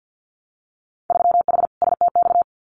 Morse code, heard as a keyed tone, about 50 wpm, 700 Hz. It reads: 3H STX
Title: Winter Field Day Exchanges 50wpm